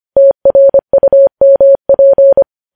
Morse code, heard as a keyed tone, about 25 wpm, 550 Hz. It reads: TRUMP
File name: podcast / News headlines-1x-25x25